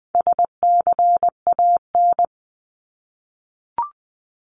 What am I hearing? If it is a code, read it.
Morse code, heard as a keyed tone, about 20 wpm, 700 Hz. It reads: SCAN
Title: 260 Lesson 42e - Words 20-20wpm